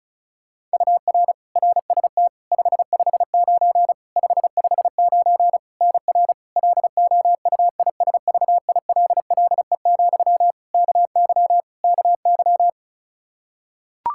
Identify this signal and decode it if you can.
Morse code, heard as a keyed tone, about 34 wpm, 700 Hz. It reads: UR RST 559 559 NR LOUISVILLE, KY KY
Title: Practice QSO - Exchange 1 - Beginning - 35wpm